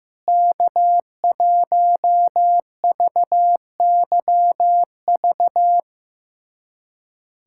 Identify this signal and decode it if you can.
Morse code, heard as a keyed tone, about 15 wpm, 700 Hz. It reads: K1VYV
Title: USA Callsign DE Callsign 15wpm